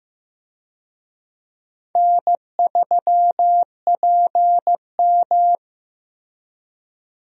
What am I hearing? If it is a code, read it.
Morse code, heard as a keyed tone, about 15 wpm, 700 Hz. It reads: N3PM
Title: USA Call Signs - 4 Characters - Repeated 2x - 15wpm